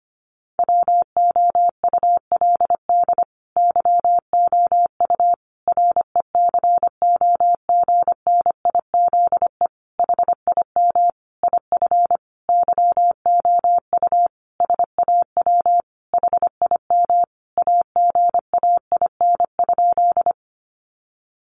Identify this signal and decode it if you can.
Morse code, heard as a keyed tone, about 25 wpm, 700 Hz. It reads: WOULD YOU RECOGNIZE HIM IF YOU SAW HIM AGAIN?
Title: Sentences from Top 1400 Words 25wpm